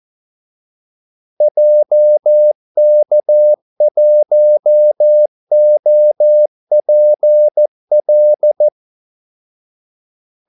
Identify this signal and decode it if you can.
Morse code, heard as a keyed tone, about 14 wpm, 600 Hz. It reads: JK1OPL
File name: Callsigns-14wpm